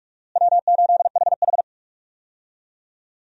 Morse code, heard as a keyed tone, about 44 wpm, 700 Hz. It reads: W8HH
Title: USA Call Signs - 4 Characters - Repeated 3x - 45wpm